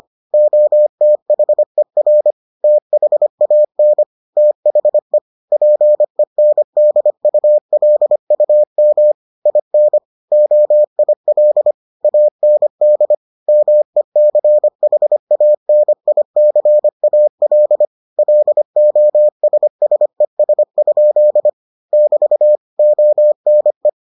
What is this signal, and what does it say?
Morse code, heard as a keyed tone, about 25 wpm, 600 Hz. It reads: OTHER THAN THE PENDULUM IN OIL AND MECHANICAL LOSSES? <BT> ONE